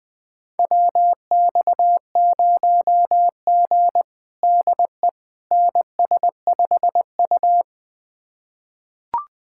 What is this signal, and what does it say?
Morse code, heard as a keyed tone, about 21 wpm, 700 Hz. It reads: WX0G DE NS5U
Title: USA Callsign DE Callsign - 4 Characters - 20wpm